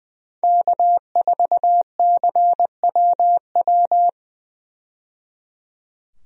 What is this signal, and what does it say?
Morse code, heard as a keyed tone, about 20 wpm, 700 Hz. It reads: K4CWW
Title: USA Call Signs - 5 Characters - 20wpm